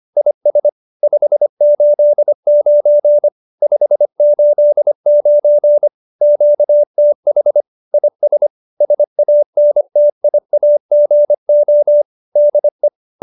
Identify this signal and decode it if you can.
Morse code, heard as a keyed tone, about 25 wpm, 590 Hz. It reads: IS 589 589 QTH IS SANTIAGO DE
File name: QSO304 25